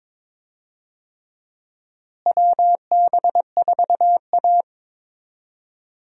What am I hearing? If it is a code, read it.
Morse code, heard as a keyed tone, about 22 wpm, 700 Hz. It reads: WB4A
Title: USA Call Signs - 4 Characters - Repeated 3x - 22wpm